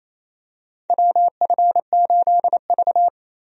Morse code, heard as a keyed tone, about 28 wpm, 690 Hz. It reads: WF8V
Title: USA Call Signs - 4 Characters - 28wpm